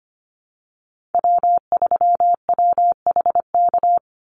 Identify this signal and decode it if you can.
Morse code, heard as a keyed tone, about 25 wpm, 700 Hz. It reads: W3WHK
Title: Call Signs - 25wpm